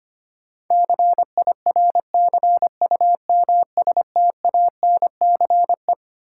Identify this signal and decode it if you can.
Morse code, heard as a keyed tone, about 25 wpm, 700 Hz. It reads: CIRCUMSTANCE
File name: Twelve Letter Words - 25wpm